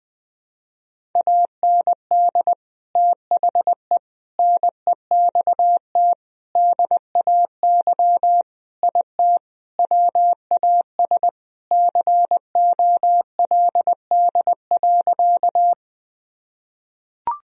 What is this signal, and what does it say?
Morse code, heard as a keyed tone, about 19 wpm, 700 Hz. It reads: AND THE NEXT DAY IT WAS COLD.
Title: Sentences from Top 1000 Words - Review Part 1 - 20wpm